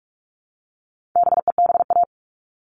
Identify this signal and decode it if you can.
Morse code, heard as a keyed tone, about 44 wpm, 700 Hz. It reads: BEBA
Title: All Cut Numbers - 4-Digit 45wpm